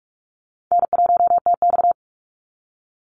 Morse code, heard as a keyed tone, about 44 wpm, 700 Hz. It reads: N1TX
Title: Easy Call Signs Repeated 3x - 45wpm